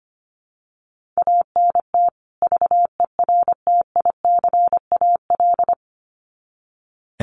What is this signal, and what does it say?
Morse code, heard as a keyed tone, about 25 wpm, 700 Hz. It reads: ANT VERTICAL
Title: Common QSO Phrases 25wpm